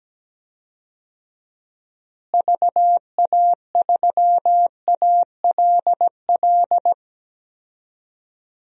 Morse code, heard as a keyed tone, about 17 wpm, 700 Hz. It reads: VA3ALL
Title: DXCC-100 Call Signs - 6 Characters - Repeated 2x - 17wpm